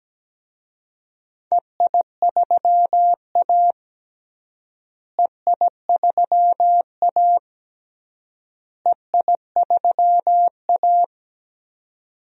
Morse code, heard as a keyed tone, about 17 wpm, 700 Hz. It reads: EI3A EI3A EI3A
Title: DXCC-100 Call Signs - 4 Characters - Repeated 3x - 17wpm